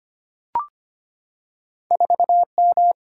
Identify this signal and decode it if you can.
Morse code, heard as a keyed tone, about 25 wpm, 700 Hz. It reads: 4M
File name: Two Letter-Number Combinations - 25wpm